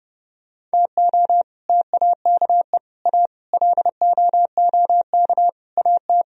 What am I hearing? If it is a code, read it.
Morse code, heard as a keyed tone, about 30 wpm, 700 Hz. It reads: TO TAKE A LOOK AT